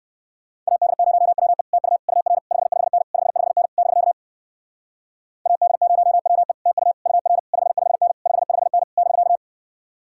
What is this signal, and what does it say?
Morse code, heard as a keyed tone, about 58 wpm, 700 Hz. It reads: AD0WE TU UR 55N 55N <BK> AD0WE TU UR 55N 55N <BK>